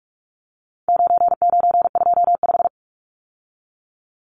Morse code, heard as a keyed tone, about 44 wpm, 700 Hz. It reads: 9925
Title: Numbers - 4-Digit 45wpm